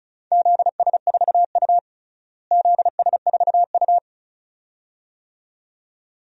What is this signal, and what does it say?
Morse code, heard as a keyed tone, about 34 wpm, 700 Hz. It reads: ZS4U ZS4U
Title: DXCC-100 Call Signs - 4 Characters - Repeated 2x - 35wpm